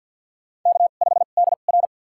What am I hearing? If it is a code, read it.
Morse code, heard as a keyed tone, about 53 wpm, spaced as for 37 wpm, 700 Hz. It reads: K5DR
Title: USA Call Signs - 4 Characters - 2x Character Spacing - 50wpm